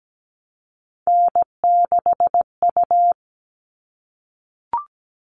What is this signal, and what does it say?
Morse code, heard as a keyed tone, about 17 wpm, 700 Hz. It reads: N6U
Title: All Cut Numbers - 3-Digit 17wpm